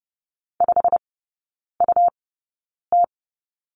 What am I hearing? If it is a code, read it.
Morse code, heard as a keyed tone, about 30 wpm, spaced as for 9 wpm, 700 Hz. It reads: 5UT
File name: Lesson 17e - Letter u - Three Characters 30wpm